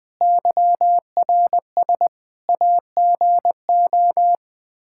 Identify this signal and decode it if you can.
Morse code, heard as a keyed tone, about 20 wpm, 700 Hz. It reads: YRS AGO
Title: Practice QSO - Exchange 3 - More Info - 20wpm